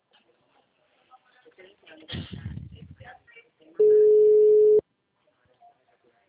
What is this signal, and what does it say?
Ringback tone (Europe, 425 Hz), with speech.